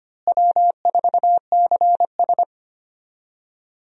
Morse code, heard as a keyed tone, about 25 wpm, 700 Hz. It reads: W4CS
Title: USA Call Signs - 4 Characters - Repeated 2x - 25wpm